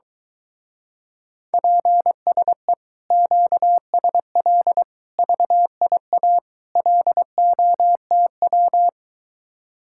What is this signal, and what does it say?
Morse code, heard as a keyed tone, about 22 wpm, 700 Hz. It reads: PSE QSL VIA LOTW